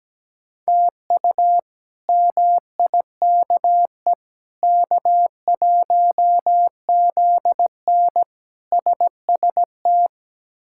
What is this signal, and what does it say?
Morse code, heard as a keyed tone, about 17 wpm, 700 Hz. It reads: TU MIKE K1ZN SST
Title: K1USN SST QSO Practice - 17wpm